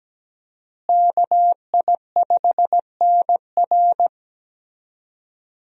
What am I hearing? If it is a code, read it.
Morse code, heard as a keyed tone, about 17 wpm, 700 Hz. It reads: KI5NR
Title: USA Call Signs - 5 Characters - Repeated 2x - 17wpm